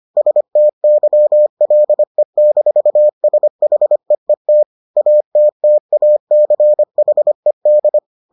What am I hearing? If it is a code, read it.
Morse code, heard as a keyed tone, about 25 wpm, 600 Hz. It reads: STYLE-SHEET ATTACHED